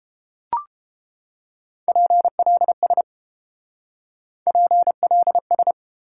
Morse code, heard as a keyed tone, about 30 wpm, 700 Hz. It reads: PLS PLS